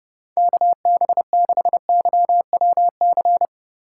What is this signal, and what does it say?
Morse code, heard as a keyed tone, about 29 wpm, 700 Hz. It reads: KB6YWC